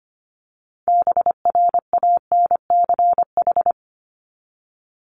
Morse code, heard as a keyed tone, about 25 wpm, 700 Hz. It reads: BRANCH
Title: Top 1751-2000 Words 25wpm